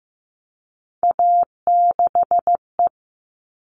Morse code, heard as a keyed tone, about 15 wpm, 700 Hz. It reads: A6E
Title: All Cut Numbers - 3-Digit 15wpm